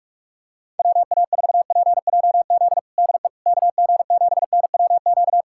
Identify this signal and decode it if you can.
Morse code, heard as a keyed tone, about 45 wpm, 690 Hz. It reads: WA4PJZ DE KG7NWQ
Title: USA Callsign DE Callsign 45wpm